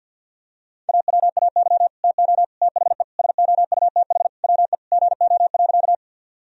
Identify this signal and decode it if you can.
Morse code, heard as a keyed tone, about 46 wpm, 700 Hz. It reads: AWAY TO THE SOUTH WE GO.
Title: Sentences from Top 1000 Words - Review Part 2 - 50wpm